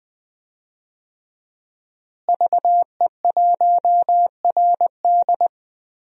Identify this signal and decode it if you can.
Morse code, heard as a keyed tone, about 20 wpm, 700 Hz. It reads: VE1RD
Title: DXCC-100 Call Signs - 5 Characters - Repeated 3x - 20wpm